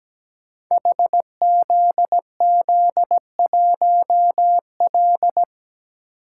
Morse code, heard as a keyed tone, about 17 wpm, 700 Hz. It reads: HZZ1L
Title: Code Groups - 5 Letter:Numbers - 17wpm